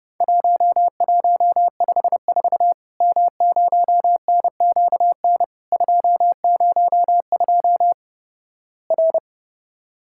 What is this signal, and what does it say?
Morse code, heard as a keyed tone, about 32 wpm, 700 Hz. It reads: 1154 M0NQN 202 R
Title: Worked All Europe DX Contest - QTC Exchange - 30wpm